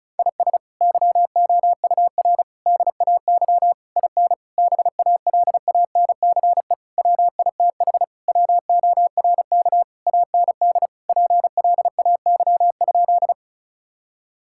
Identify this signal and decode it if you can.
Morse code, heard as a keyed tone, about 34 wpm, 700 Hz. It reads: IS YOUR DAY IN BALANCE WITH WORK AND PLAY?
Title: Sentences from Top 1000 Words - Review Part 6 - 35wpm